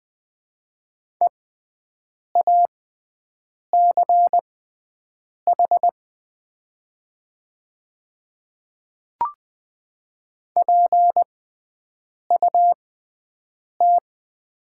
Morse code, heard as a keyed tone, about 20 wpm, spaced as for 7 wpm, 690 Hz. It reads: EACH PUT
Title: 6x Character Spacing - Top 200 Words - 20wpm